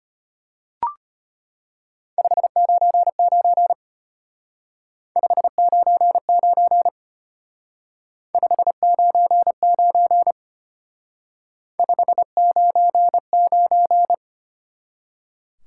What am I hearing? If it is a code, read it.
Morse code, heard as a keyed tone, about 31 wpm, 700 Hz. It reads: 599 599 599 599
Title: Speed-Racing Top 180 QSO Elements 25wpm